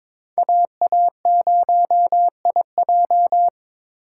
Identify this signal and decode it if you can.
Morse code, heard as a keyed tone, about 26 wpm, 700 Hz. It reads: AA0IJ
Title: Speed Racing - USA 2x2 Call Signs 22wpm